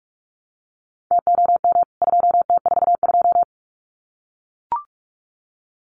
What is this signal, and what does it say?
Morse code, heard as a keyed tone, about 44 wpm, 700 Hz. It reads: TOM 2T42
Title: CWT Exchange 45wpm